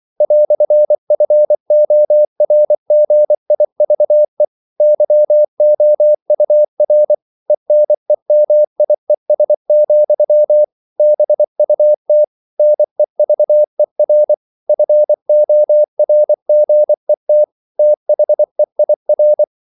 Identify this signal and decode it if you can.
Morse code, heard as a keyed tone, about 24 wpm, 600 Hz. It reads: "FORGIVE YOUR ENEMIES, BUT NEVER FORGET THEIR
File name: CW / Practice / EnglishQuotations0324